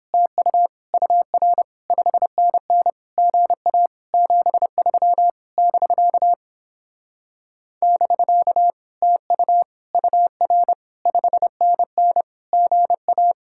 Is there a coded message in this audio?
Yes: Morse code from 0.1 to 13.3 s, heard as a keyed tone, about 29 wpm, 700 Hz. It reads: TU UR 5NN GA 73 <BK> <BK> TU UR 5NN GA